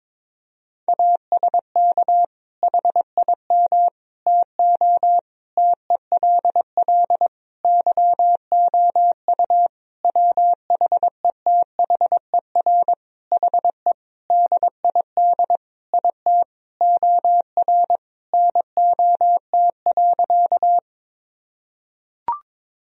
Morse code, heard as a keyed tone, about 22 wpm, 700 Hz. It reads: ASK HIM TO TELL YOU WHETHER HE DID IT OR NOT.
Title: Sentences from Top 1000 Words - Review Part 3 - 22wpm